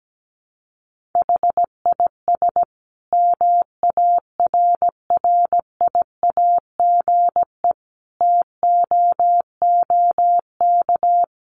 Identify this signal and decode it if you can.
Morse code, heard as a keyed tone, about 17 wpm, 700 Hz. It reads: HIS MARRIAGE TOOK
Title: Sentences from Top 1500 Words 17wpm